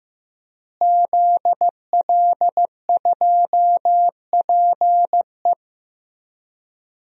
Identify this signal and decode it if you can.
Morse code, heard as a keyed tone, about 15 wpm, 700 Hz. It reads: ZL2PE
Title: DXCC-100 Call Signs - 5 Characters - Repeated 3x - 15wpm